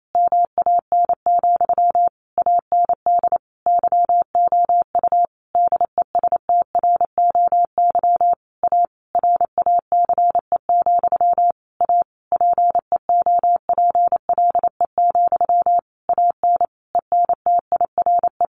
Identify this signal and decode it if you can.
Morse code, heard as a keyed tone, about 28 wpm, 700 Hz. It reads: MAN, AND YOU DESTROY A RACE, A PEOPLE, AN ENTIRE
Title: A Sound Of Thunder 28wpm